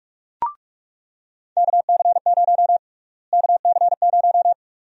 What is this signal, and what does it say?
Morse code, heard as a keyed tone, about 46 wpm, 700 Hz. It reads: KC0 KC0
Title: USA Call Sign Prefixes - Repeated 2x - 45wpm